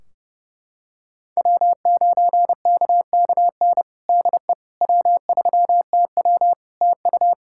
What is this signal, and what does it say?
Morse code, heard as a keyed tone, about 30 wpm, 700 Hz. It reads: W9KKN DE W3TW TU
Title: Practice QSO - Exchange 4 - Ending - 30wpm